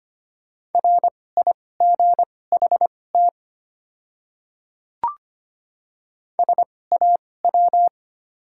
Morse code, heard as a keyed tone, about 25 wpm, spaced as for 18 wpm, 700 Hz. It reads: RIGHT SAW
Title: 2x Character Spacing - Top 300 Words - 25wpm